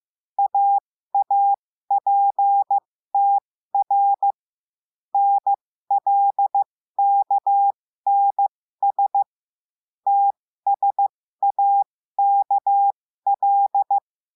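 Morse code, heard as a keyed tone, about 15 wpm, spaced as for 13 wpm, 800 Hz. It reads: AAPTR NLKNS TSAKL
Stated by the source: Morse lesson adding the letter L, five-letter random groups.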